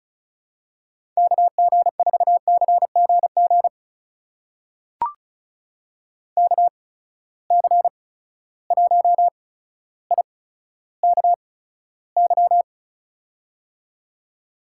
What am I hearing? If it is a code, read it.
Morse code, heard as a keyed tone, about 36 wpm, spaced as for 15 wpm, 700 Hz. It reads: KG4CGG KC1IKY
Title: USA Call Signs - 6 Characters - 8x Character Spacing - 35wpm